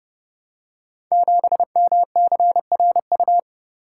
Morse code, heard as a keyed tone, about 28 wpm, 700 Hz. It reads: 7MCRU
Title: Code Groups - 5 Letter:Numbers - 30wpm